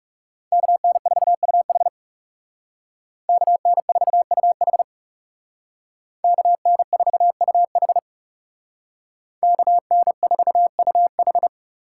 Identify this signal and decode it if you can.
Morse code, heard as a keyed tone, about 38 wpm, 700 Hz. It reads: KN4UH KN4UH KN4UH KN4UH